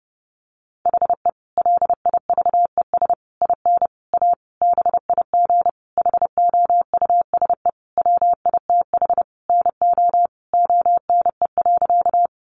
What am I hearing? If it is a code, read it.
Morse code, heard as a keyed tone, about 30 wpm, 700 Hz. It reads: HE LIVES IN A BIG HOUSE WITH NO ONE.
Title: Additional Easy Phrases 30wpm